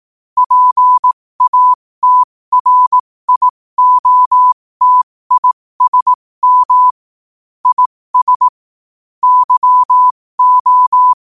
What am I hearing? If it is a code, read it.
Morse code, heard as a keyed tone, about 18 wpm, spaced as for 15 wpm, 1000 Hz. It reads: PATRIOTISM IS YO
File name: qotdfiles / 15wpm